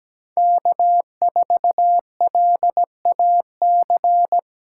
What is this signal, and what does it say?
Morse code, heard as a keyed tone, about 20 wpm, 700 Hz. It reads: K4LAC
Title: Speed Racing - USA Call Signs 17wpm